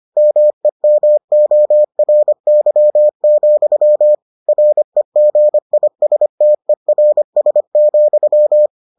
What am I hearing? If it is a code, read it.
Morse code, heard as a keyed tone, about 25 wpm, 600 Hz. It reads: MEMORY, REGISTERS,